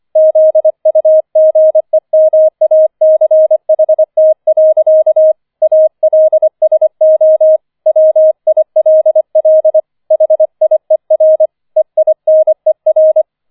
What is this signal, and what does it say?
Morse code, heard as a keyed tone, about 25 wpm, 630 Hz. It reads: ZUGEMACHT. ALSO WILL HIER EINER